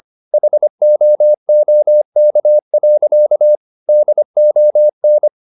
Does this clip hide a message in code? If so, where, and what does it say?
Yes: Morse code from 0.3 to 5.3 s, heard as a keyed tone, about 25 wpm, 600 Hz. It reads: HOOK. DON